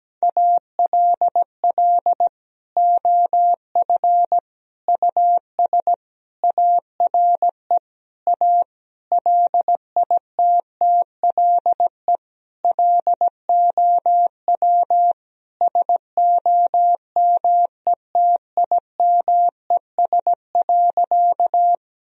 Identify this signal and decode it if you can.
Morse code, heard as a keyed tone, about 17 wpm, 700 Hz. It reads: ALL OF US ARE A LITTLE LOW SOMETIMES.
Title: Sentences from Top 1000 Words - Review Part 3 - 17wpm